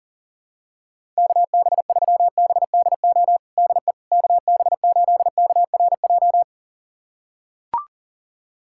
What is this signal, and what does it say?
Morse code, heard as a keyed tone, about 40 wpm, 700 Hz. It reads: KB3BDO DE KB8KRJ
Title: USA Callsign DE Callsign 40wpm